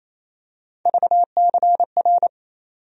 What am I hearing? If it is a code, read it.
Morse code, heard as a keyed tone, about 28 wpm, 700 Hz. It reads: VCR
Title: USA Call Sign Postfixes - 3 Letters 28wpm